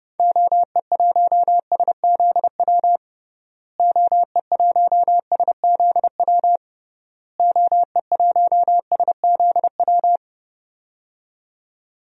Morse code, heard as a keyed tone, about 32 wpm, 700 Hz. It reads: OE1SZW OE1SZW OE1SZW
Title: DXCC-100 Call Signs - 6 Characters - Repeated 3x - 30wpm